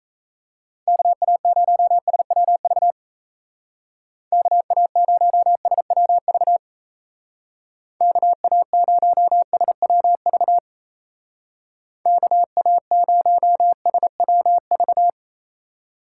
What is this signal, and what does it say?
Morse code, heard as a keyed tone, about 35 wpm, 700 Hz. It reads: KA0SWV KA0SWV KA0SWV KA0SWV
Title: Speed Racing - USA 2x3 Call Signs 28wpm